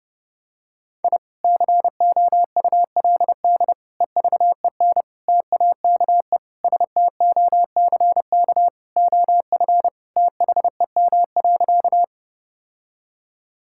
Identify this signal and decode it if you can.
Morse code, heard as a keyed tone, about 30 wpm, 700 Hz. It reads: I COULD EVEN TAKE STOCK OF THEM.
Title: Sentences from Top 1000 Words - Review Part 5 - 30wpm